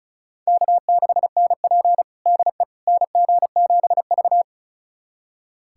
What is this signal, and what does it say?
Morse code, heard as a keyed tone, about 36 wpm, 700 Hz. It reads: K6NP DE NG7V
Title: USA Callsign DE Callsign - 4 Characters - 35wpm